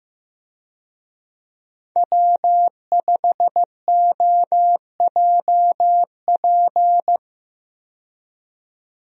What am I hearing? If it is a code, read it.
Morse code, heard as a keyed tone, about 16 wpm, 700 Hz. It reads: W5OJP
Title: USA Call Signs - 5 Characters - 15wpm